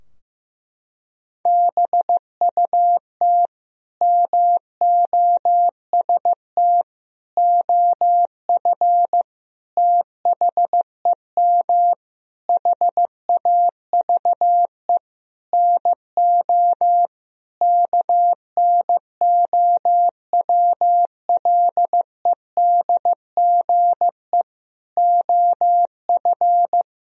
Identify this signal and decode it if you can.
Morse code, heard as a keyed tone, about 15 wpm, 700 Hz. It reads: BUT MOST OF THEM HAVE NO KNOWLEDGE OF
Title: Sentences from Top 1000 Words - Review Part 3 - 15wpm